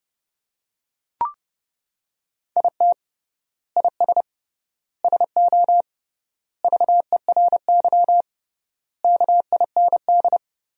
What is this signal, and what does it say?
Morse code, heard as a keyed tone, about 30 wpm, 700 Hz. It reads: IT IS SO VERY KIND
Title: 3x Word-Spacing - Sentences from Top 300 Words - 30wpm